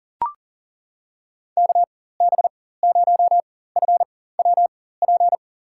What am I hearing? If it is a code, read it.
Morse code, heard as a keyed tone, about 37 wpm, spaced as for 18 wpm, 700 Hz. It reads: KB0FWP
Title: USA Call Signs - 6 Characters - 4x Character Spacing - 40wpm